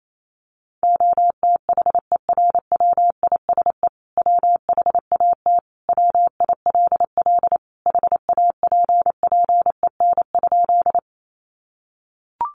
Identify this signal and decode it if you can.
Morse code, heard as a keyed tone, about 28 wpm, 700 Hz. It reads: OTHERWISE WHAT WILL HAPPEN?
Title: Sentences from Top 1200 Words 28wpm